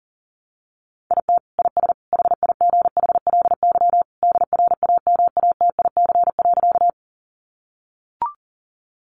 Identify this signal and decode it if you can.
Morse code, heard as a keyed tone, about 40 wpm, 700 Hz. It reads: IT IS HIGHLY DRAMATIC.
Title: Sentences from Top 1300 Words 40wpm